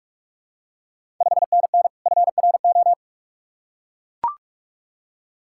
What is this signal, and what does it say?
Morse code, heard as a keyed tone, about 44 wpm, 700 Hz. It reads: 5NN FRO
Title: Ontario QSO Party Exchange 45wpm